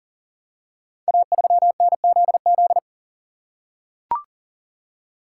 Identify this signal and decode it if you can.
Morse code, heard as a keyed tone, about 43 wpm, 700 Hz. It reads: A3NZZ
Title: Code Groups - 5 Letter:Numbers - 40wpm